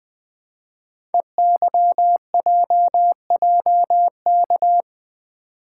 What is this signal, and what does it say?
Morse code, heard as a keyed tone, about 20 wpm, 700 Hz. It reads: EYJJK
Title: Code Groups - 5 Letters - 20wpm